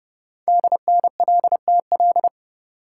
Morse code, heard as a keyed tone, about 30 wpm, 700 Hz. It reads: DNLTL
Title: Code Groups - 5 Letters - 30wpm